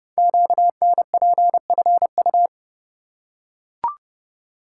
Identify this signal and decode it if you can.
Morse code, heard as a keyed tone, about 30 wpm, 700 Hz. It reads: QNPFU
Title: Code Groups - 5 Letters - 30wpm